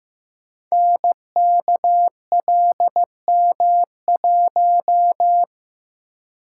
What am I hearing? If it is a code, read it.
Morse code, heard as a keyed tone, about 15 wpm, 700 Hz. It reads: NKLM1
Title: Code Groups - 5 Letter:Numbers - 15wpm